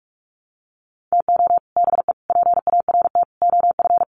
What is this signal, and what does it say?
Morse code, heard as a keyed tone, about 44 wpm, 700 Hz. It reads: TO BE PART OF
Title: Sets of 4 Words 45wpm